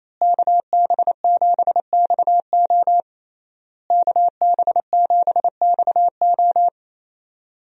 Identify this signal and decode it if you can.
Morse code, heard as a keyed tone, about 27 wpm, 700 Hz. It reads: KB7XO KB7XO
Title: USA Call Signs - 5 Characters - Repeated 3x - 28wpm